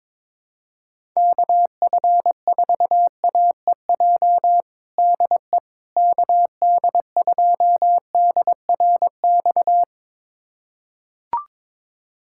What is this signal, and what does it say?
Morse code, heard as a keyed tone, about 22 wpm, 700 Hz. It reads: KF4AEJ DE KD2DRX
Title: USA Callsign DE Callsign - 6 Characters - 22wpm